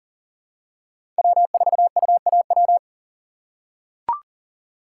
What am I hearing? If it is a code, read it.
Morse code, heard as a keyed tone, about 40 wpm, 700 Hz. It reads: W4UAW
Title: USA Call Signs - 5 Characters - 4x Character Spacing - 40wpm